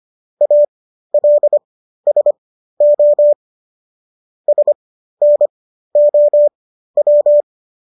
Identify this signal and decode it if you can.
Morse code, heard as a keyed tone, about 25 wpm, spaced as for 13 wpm, 590 Hz. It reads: ALSO SNOW